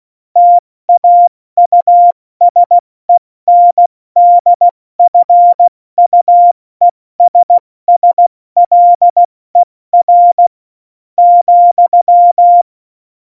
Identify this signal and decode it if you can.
Morse code, heard as a keyed tone, about 16 wpm, spaced as for 14 wpm, 700 Hz. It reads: TAUSENDFUESSLER,
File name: Morsekurs / Witze / Witz 27 March 2018, 70 F80